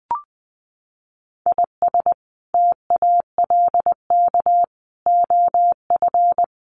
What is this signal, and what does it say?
Morse code, heard as a keyed tone, about 20 wpm, 700 Hz. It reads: IS TALK OF
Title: Sets of 5 Words Encore2 20wpm